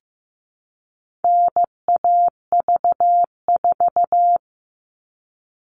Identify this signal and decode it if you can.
Morse code, heard as a keyed tone, about 15 wpm, 700 Hz. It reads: NAV4
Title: All Cut Numbers - 4-Digit 15wpm